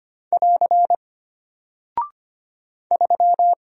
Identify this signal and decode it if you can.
Morse code, heard as a keyed tone, about 25 wpm, 700 Hz. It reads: <AR> 3
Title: Single Character (Letter:Number:Punctuation:Prosign) Alt 25wpm